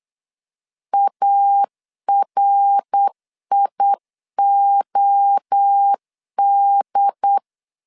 Morse code, heard as a keyed tone, about 8 wpm, 800 Hz. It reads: ARIOD